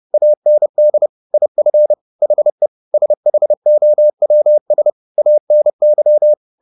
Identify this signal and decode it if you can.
Morse code, heard as a keyed tone, about 30 wpm, 600 Hz. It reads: AND IF HE SHOWS ANY